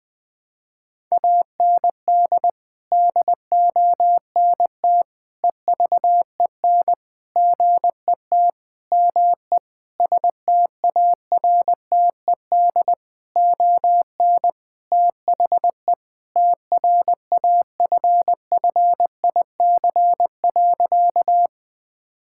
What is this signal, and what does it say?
Morse code, heard as a keyed tone, about 21 wpm, 700 Hz. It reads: AND DONT EVEN GET ME STARTED ON THE TRAFFIC.
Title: Sentences with 8-Word Phrases - 20wpm